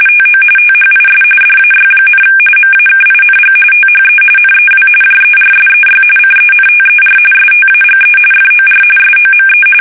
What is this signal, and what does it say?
RTTY, about 45.45 baud. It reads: TO THIS DAY, MAKING THE NEW JERSEY CENTER STAGE FOR SOME
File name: fsk850 / battleshipnj